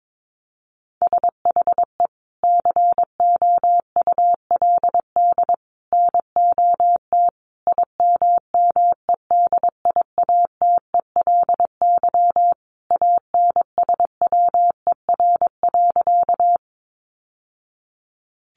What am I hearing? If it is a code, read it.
Morse code, heard as a keyed tone, about 22 wpm, 700 Hz. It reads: SHE COULD NOT IMMEDIATELY ANSWER.
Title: Sentences from Top 1100 Words 22wpm